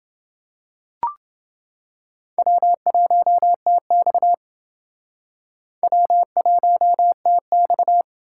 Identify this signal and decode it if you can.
Morse code, heard as a keyed tone, about 29 wpm, 700 Hz. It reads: W1TX W1TX